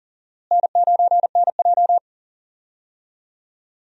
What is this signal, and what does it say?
Morse code, heard as a keyed tone, about 39 wpm, 700 Hz. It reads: N9NJ
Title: USA Call Signs - 4 Characters - Repeated 2x - 40wpm